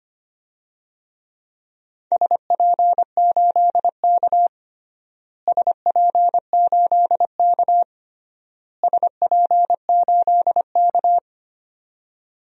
Morse code, heard as a keyed tone, about 25 wpm, 700 Hz. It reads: SP8K SP8K SP8K
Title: DXCC-100 Call Signs - 4 Characters - Repeated 3x - 25wpm